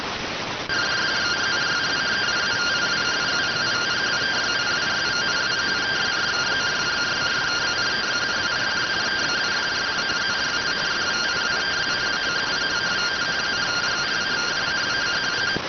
Two-tone teleprinter signal, about 170 Hz shift.